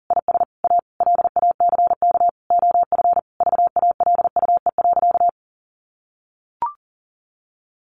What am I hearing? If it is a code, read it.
Morse code, heard as a keyed tone, about 37 wpm, 700 Hz. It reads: IS A LACK OF VALUE.
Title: Sentences from Top 1100 Words 40wpm